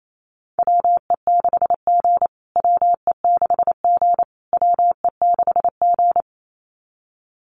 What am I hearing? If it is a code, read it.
Morse code, heard as a keyed tone, about 28 wpm, 700 Hz. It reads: WE6G WE6G WE6G
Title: USA Call Signs Repeated 3x 28wpm